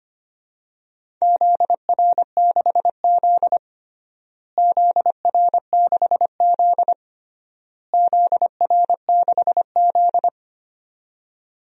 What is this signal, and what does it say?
Morse code, heard as a keyed tone, about 25 wpm, 700 Hz. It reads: ZR6Z ZR6Z ZR6Z
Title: DXCC-100 Call Signs - 4 Characters - Repeated 3x - 25wpm